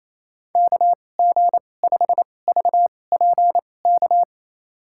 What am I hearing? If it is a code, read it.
Morse code, heard as a keyed tone, about 28 wpm, spaced as for 20 wpm, 700 Hz. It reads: KG5VPK
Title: USA Call Signs - 2x Character Spacing - 28wpm